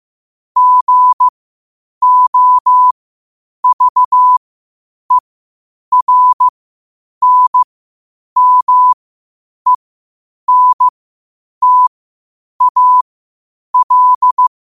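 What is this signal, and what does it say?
Morse code, heard as a keyed tone, about 15 wpm, spaced as for 8 wpm, 1000 Hz. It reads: GOVERNMENTAL
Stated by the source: Morse practice at 8 WPM from Morse Resource.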